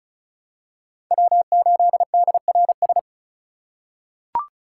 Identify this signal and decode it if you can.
Morse code, heard as a keyed tone, about 34 wpm, 700 Hz. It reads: W8DRS
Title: USA Call Signs - 5 Characters - 2x Character Spacing - 35wpm